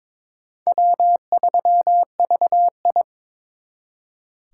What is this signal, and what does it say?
Morse code, heard as a keyed tone, about 26 wpm, 700 Hz. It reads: W3VI
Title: Speed Racing - USA Call Signs 22wpm